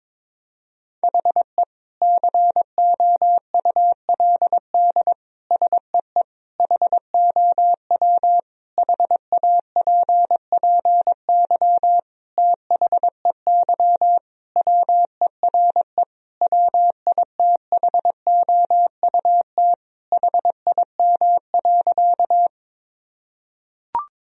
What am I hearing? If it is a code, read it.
Morse code, heard as a keyed tone, about 22 wpm, 700 Hz. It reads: HE COULD SEE HOW HAPPY THEY WERE WITHOUT HIM.
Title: Sentences from Top 1000 Words - Review Part 3 - 22wpm